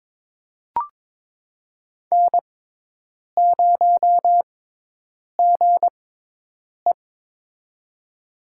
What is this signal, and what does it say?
Morse code, heard as a keyed tone, about 22 wpm, spaced as for 8 wpm, 700 Hz. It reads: N0GE
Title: USA Call Signs - 4 Characters - 6x Character Spacing - 22wpm